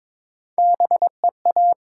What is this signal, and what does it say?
Morse code, heard as a keyed tone, about 22 wpm, 700 Hz. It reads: B E A
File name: Top 90 QSO Elements - 6x Character Spacing - 22wpm